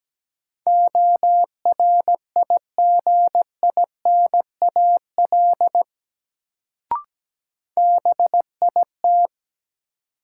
Morse code, heard as a keyed tone, about 17 wpm, 700 Hz. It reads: ORIGINAL BIT
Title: Sentences from Top 900 Words - New Word Drill - 17wpm